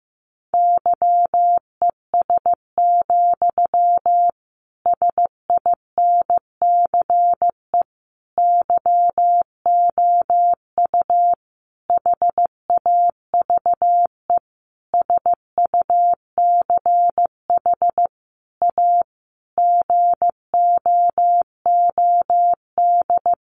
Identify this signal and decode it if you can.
Morse code, heard as a keyed tone, about 15 wpm, 700 Hz. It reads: YES, SINCE YOU HAVE SUCH A GOOD
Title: Sentences from Top 1300 Words 15wpm